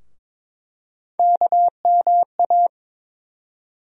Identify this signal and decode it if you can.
Morse code, heard as a keyed tone, about 22 wpm, 700 Hz. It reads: KMA
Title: USA Call Sign Postfixes - 3 Letters 22wpm